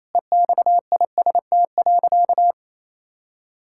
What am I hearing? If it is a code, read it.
Morse code, heard as a keyed tone, about 28 wpm, 700 Hz. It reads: EXIST.
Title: Sentences from Top 1000 Words - Review Part 5 - 28wpm